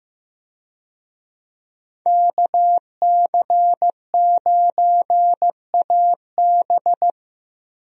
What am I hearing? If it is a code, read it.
Morse code, heard as a keyed tone, about 16 wpm, 700 Hz. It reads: KC9AB
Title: USA Call Signs - 5 Characters - Repeated 3x - 15wpm